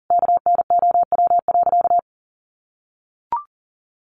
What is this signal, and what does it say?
Morse code, heard as a keyed tone, about 40 wpm, 700 Hz. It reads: KNOW.
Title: Sentences from Top 1200 Words 40wpm